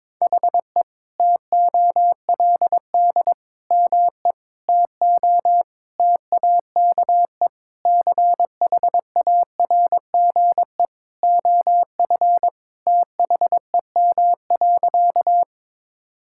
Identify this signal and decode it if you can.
Morse code, heard as a keyed tone, about 22 wpm, 700 Hz. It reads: HE TOLD ME TO TAKE CHARGE OF THEM.
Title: Sentences from Top 1000 Words - Review Part 6 - 22wpm